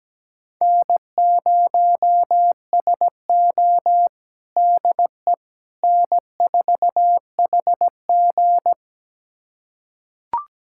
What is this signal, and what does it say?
Morse code, heard as a keyed tone, about 17 wpm, 700 Hz. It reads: N0SO DE N4HG
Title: USA Callsign DE Callsign - 4 Characters - 17wpm